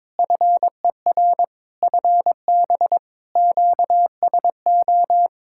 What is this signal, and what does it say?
Morse code, heard as a keyed tone, about 22 wpm, 700 Hz. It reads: FER FB QSO
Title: K1USN SST QSO Practice 22wpm